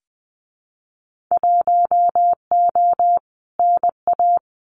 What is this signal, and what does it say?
Morse code, heard as a keyed tone, about 19 wpm, 700 Hz. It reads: 1O NA
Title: Winter Field Day Exchanges 20wpm